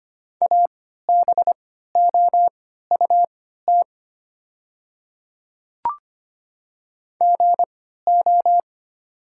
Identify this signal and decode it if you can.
Morse code, heard as a keyed tone, about 25 wpm, spaced as for 14 wpm, 690 Hz. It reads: ABOUT GO
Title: 3x Character Spacing - Top 200 Words - 25wpm